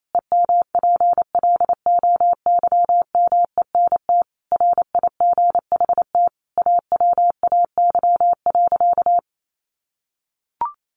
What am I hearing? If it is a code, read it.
Morse code, heard as a keyed tone, about 28 wpm, 700 Hz. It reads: EMPLOYMENT RIGHT AWAY.
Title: Sentences from Top 1500 Words 28wpm